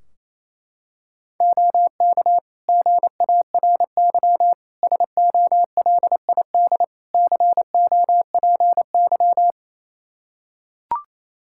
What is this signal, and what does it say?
Morse code, heard as a keyed tone, about 28 wpm, 700 Hz. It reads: OK GARY SOLID COPY
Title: Practice QSO - Exchange 3 - More Info - 28wpm